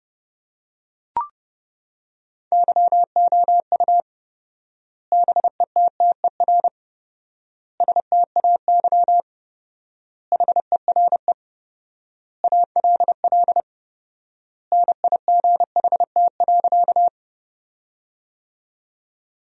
Morse code, heard as a keyed tone, about 32 wpm, 700 Hz. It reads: YOU BETTER STAY HERE ALL NIGHT.
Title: 4x Word-Spacing - Sentences from Top 500 Words - 30wpm